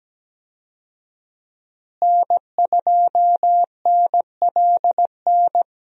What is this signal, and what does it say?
Morse code, heard as a keyed tone, about 17 wpm, 700 Hz. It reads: N2NLN
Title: USA Call Signs - 5 Characters - Repeated 2x - 17wpm